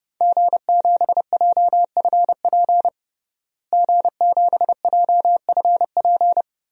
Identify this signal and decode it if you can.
Morse code, heard as a keyed tone, about 32 wpm, 700 Hz. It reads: G7JFP G7JFP
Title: DXCC-100 Call Signs - 5 Characters - Repeated 2x - 30wpm